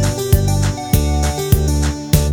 There is no code.